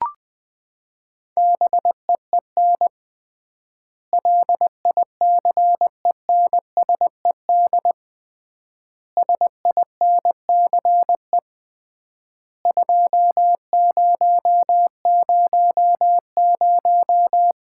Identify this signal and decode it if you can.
Morse code, heard as a keyed tone, about 21 wpm, 700 Hz. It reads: BEEN LICENSED SINCE 2000
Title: Common QSO Phrases - 3x Word-Spacing - 20wpm